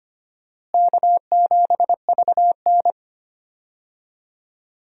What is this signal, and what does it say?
Morse code, heard as a keyed tone, about 25 wpm, 700 Hz. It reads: K7VN
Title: USA Call Signs - 4 Characters - Repeated 3x - 25wpm